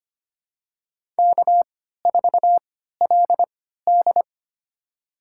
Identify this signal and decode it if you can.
Morse code, heard as a keyed tone, about 25 wpm, spaced as for 14 wpm, 700 Hz. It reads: K4LD
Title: USA Call Signs - 3x Character Spacing - 25wpm